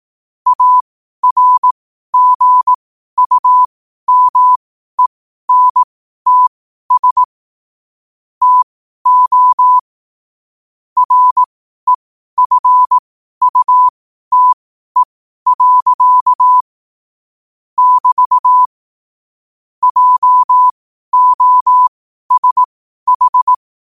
Morse code, heard as a keyed tone, about 18 wpm, spaced as for 12 wpm, 1000 Hz. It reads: ARGUMENTS TO REFUTE. <BT> JOSH
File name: qotdfiles / 12wpm